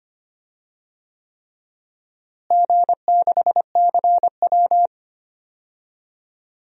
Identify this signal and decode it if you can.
Morse code, heard as a keyed tone, about 25 wpm, 700 Hz. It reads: G6CW
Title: DXCC-100 Call Signs - 4 Characters - Repeated 3x - 25wpm